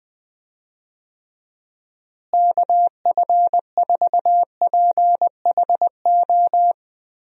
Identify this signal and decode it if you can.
Morse code, heard as a keyed tone, about 20 wpm, 700 Hz. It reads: KF4PHO
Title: USA Call Signs - 6 Characters - Repeated 3x - 20wpm